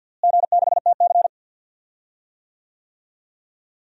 Morse code, heard as a keyed tone, about 52 wpm, 700 Hz. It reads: G6TC
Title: DXCC-100 Call Signs - 4 Characters - Repeated 2x - 50wpm